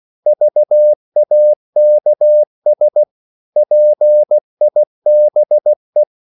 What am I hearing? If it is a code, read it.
Morse code, heard as a keyed tone, about 16 wpm, 600 Hz. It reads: VAKS PIBE
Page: Korte danske ord 16wpm